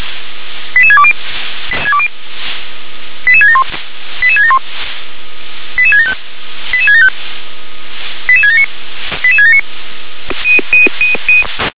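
Five-tone selective call (ZVEI), heard as quick runs of five tones, tones of about 70 ms. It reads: 88620 88671 88671 88675 88675 88679 88679
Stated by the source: ZVEI five-tone selective call 88776 received on SDR.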